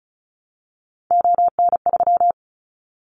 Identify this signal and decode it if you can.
Morse code, heard as a keyed tone, about 36 wpm, 700 Hz. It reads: ON3
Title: Callsign Prefixes for Top 50 DXCC - 35wpm